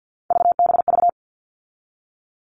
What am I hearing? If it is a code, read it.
Morse code, heard as a keyed tone, about 50 wpm, 700 Hz. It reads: VBV
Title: All Cut Numbers - 3-Digit 50wpm